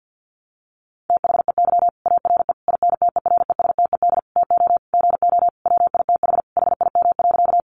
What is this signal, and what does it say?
Morse code, heard as a keyed tone, about 50 wpm, 700 Hz. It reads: THEY ARE INTERESTED TO GO WITH HIM.
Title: Sentences from Top 900 Words 50wpm